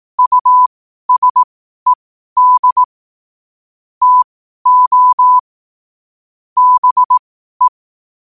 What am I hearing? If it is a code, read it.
Morse code, heard as a keyed tone, about 18 wpm, spaced as for 12 wpm, 1000 Hz. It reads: USED TO BE
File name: qotdfiles / 12wpm